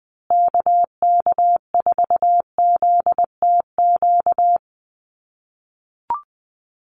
Morse code, heard as a keyed tone, about 20 wpm, 700 Hz. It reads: KK4ZTQ
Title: USA Call Signs Repeated 3x 20wpm